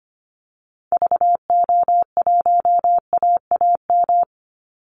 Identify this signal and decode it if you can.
Morse code, heard as a keyed tone, about 25 wpm, 700 Hz. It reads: VO1AAM
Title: Call Signs Repeated 3x - 25wpm